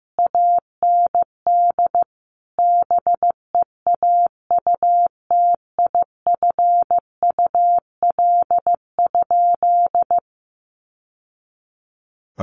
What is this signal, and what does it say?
Morse code, heard as a keyed tone, about 15 wpm, 700 Hz. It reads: AND BEAUTIFUL?
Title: Sentences from Top 1200 Words 15wpm